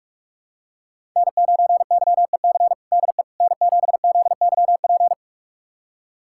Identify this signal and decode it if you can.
Morse code, heard as a keyed tone, about 46 wpm, 700 Hz. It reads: N9YEC DE N7ZYP
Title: USA Callsign DE Callsign - 5 Characters - 45wpm